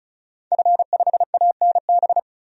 Morse code, heard as a keyed tone, about 35 wpm, 700 Hz. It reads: F5ANB
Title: DXCC-100 Call Signs - 5 Characters - 35wpm